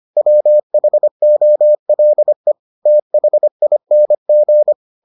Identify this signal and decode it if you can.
Morse code, heard as a keyed tone, about 25 wpm, 600 Hz. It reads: WHOLE THING